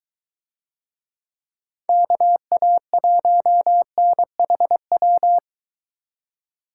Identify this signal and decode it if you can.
Morse code, heard as a keyed tone, about 22 wpm, 700 Hz. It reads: KA1NHW